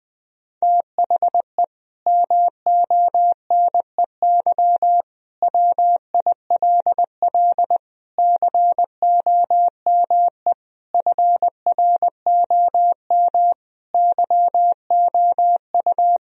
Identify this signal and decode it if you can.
Morse code, heard as a keyed tone, about 21 wpm, 700 Hz. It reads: THE MONEY WILL COME FROM YOU
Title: Sentences from Top 1000 Words - Review Part 4 - 20wpm